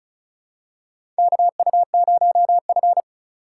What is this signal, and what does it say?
Morse code, heard as a keyed tone, about 36 wpm, 700 Hz. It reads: KU0F
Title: USA Call Signs - 2x Character Spacing - 35wpm